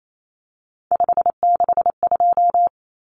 Morse code, heard as a keyed tone, about 28 wpm, 700 Hz. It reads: 562
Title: Numbers - 3-Digit 28wpm